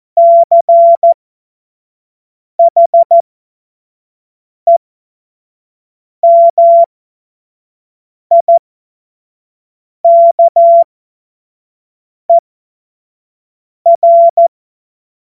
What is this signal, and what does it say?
Morse code, heard as a keyed tone, about 14 wpm, spaced as for 5 wpm, 680 Hz. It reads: CHEMIKER
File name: Morsekurs / Witze / Witz 06. Januar 2026 25 F70